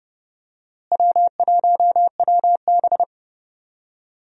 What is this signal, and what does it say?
Morse code, heard as a keyed tone, about 30 wpm, 700 Hz. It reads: W1WB
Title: USA Call Signs - 4 Characters - Repeated 3x - 30wpm